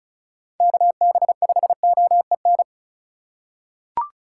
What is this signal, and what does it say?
Morse code, heard as a keyed tone, about 36 wpm, 700 Hz. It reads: KB5OEN
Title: USA Call Signs - 6 Characters - 3x Character Spacing - 35wpm